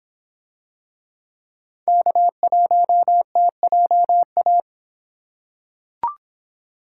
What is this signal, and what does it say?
Morse code, heard as a keyed tone, about 26 wpm, 700 Hz. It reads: K1TJA